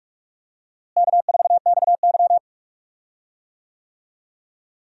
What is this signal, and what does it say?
Morse code, heard as a keyed tone, about 42 wpm, 700 Hz. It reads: K4XY
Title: USA Call Signs - 4 Characters - Repeated 3x - 45wpm